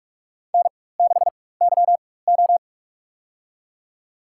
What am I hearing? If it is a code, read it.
Morse code, heard as a keyed tone, about 46 wpm, spaced as for 21 wpm, 700 Hz. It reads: N6YO
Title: USA Call Signs - 4 Characters - 4x Character Spacing - 45wpm